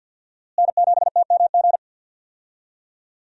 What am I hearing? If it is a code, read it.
Morse code, heard as a keyed tone, about 53 wpm, 700 Hz. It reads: N7TMG
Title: USA Call Signs - 5 Characters - Repeated 3x - 50wpm